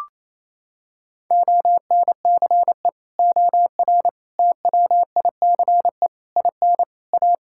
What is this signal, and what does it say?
Morse code, heard as a keyed tone, about 28 wpm, 700 Hz. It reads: ONCE OR TWICE IN A
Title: Sentences from Top 1600 Words 28wpm